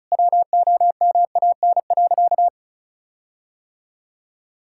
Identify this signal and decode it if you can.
Morse code, heard as a keyed tone, about 34 wpm, 700 Hz. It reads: WOMAN.
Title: Sentences from Top 1600 Words 35wpm